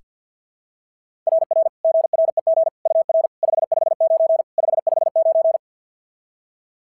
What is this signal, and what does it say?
Morse code, heard as a keyed tone, about 48 wpm, 660 Hz. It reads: RR GREG UR 559 559
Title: SOTA QSO Practice 50wpm